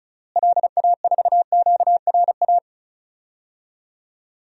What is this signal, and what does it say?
Morse code, heard as a keyed tone, about 36 wpm, 700 Hz. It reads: LA4QRA
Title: DXCC-100 Call Signs - 6 Characters - Repeated 3x - 35wpm